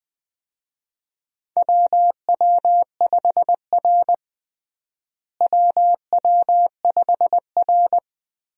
Morse code, heard as a keyed tone, about 20 wpm, 700 Hz. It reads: WW5R WW5R
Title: USA Call Signs - 4 Characters - Repeated 2x - 20wpm